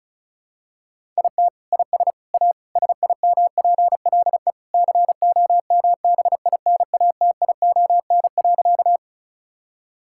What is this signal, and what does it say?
Morse code, heard as a keyed tone, about 36 wpm, 700 Hz. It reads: IT IS A SIMPLE COMBINATION.
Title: Sentences from Top 1000 Words - Review Part 6 - 35wpm